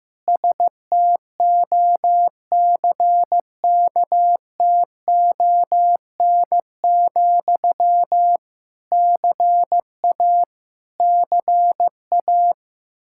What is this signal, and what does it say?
Morse code, heard as a keyed tone, about 15 wpm, 700 Hz. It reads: STOCKTON, CA CA
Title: Practice QSO - Exchange 1 - Beginning - 15wpm